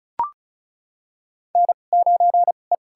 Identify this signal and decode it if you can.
Morse code, heard as a keyed tone, about 36 wpm, spaced as for 25 wpm, 700 Hz. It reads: N9E
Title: USA Call Signs - 2x Character Spacing - 35wpm